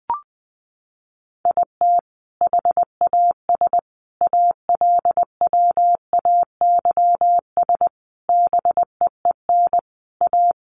Morse code, heard as a keyed tone, about 20 wpm, 700 Hz. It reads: IT HAS ALWAYS BEEN A
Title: Sentences from Top 1300 Words 20wpm